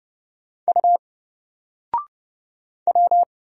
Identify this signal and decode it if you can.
Morse code, heard as a keyed tone, about 29 wpm, 700 Hz. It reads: U W
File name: Single-Letter-Number-30wpm